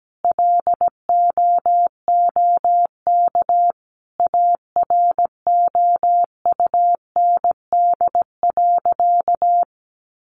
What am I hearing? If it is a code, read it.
Morse code, heard as a keyed tone, about 17 wpm, 700 Hz. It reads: LOOK AROUND.
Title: Sentences from Top 600 Words 17wpm